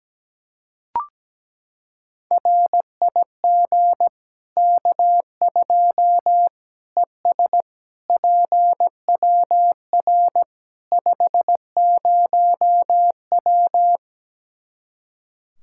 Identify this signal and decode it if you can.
Morse code, heard as a keyed tone, about 17 wpm, 700 Hz. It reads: RIG K2 ES PWR 50W
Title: Practice QSO - Exchange 2 - Info - 17wpm